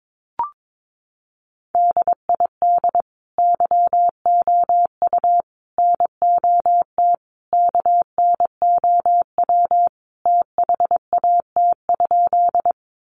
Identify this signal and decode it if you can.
Morse code, heard as a keyed tone, about 22 wpm, 700 Hz. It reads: DID YOU NOT KNOW THAT?
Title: Sentences from Top 600 Words 22wpm